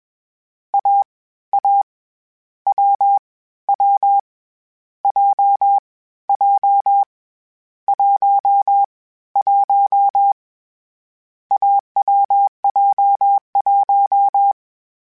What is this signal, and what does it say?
Morse code, heard as a keyed tone, about 21 wpm, 800 Hz. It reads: A A W W J J 1 1 AWJ1
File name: Difficult Combos / Difficult Combos-mp3s / awj1 ndb6 gz7 @20